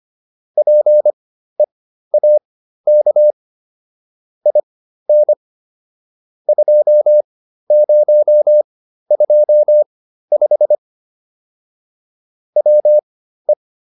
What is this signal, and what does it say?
Morse code, heard as a keyed tone, about 25 wpm, spaced as for 13 wpm, 600 Hz. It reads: PEAK IN 2025 WE